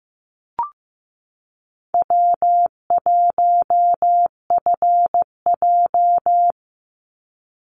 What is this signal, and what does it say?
Morse code, heard as a keyed tone, about 15 wpm, 700 Hz. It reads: W1FJ
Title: Easy Call Signs Repeated 3x - 15wpm